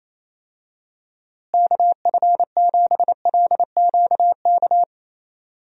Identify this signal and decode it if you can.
Morse code, heard as a keyed tone, about 28 wpm, 700 Hz. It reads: KF7LQK
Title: USA Call Signs - 6 Characters - Repeated 3x - 28wpm